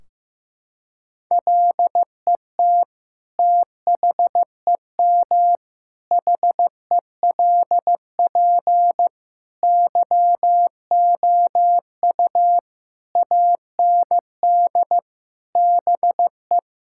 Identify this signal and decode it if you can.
Morse code, heard as a keyed tone, about 15 wpm, 700 Hz. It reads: LET THEM HELP YOU AND BE
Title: Sentences from Top 1600 Words 15wpm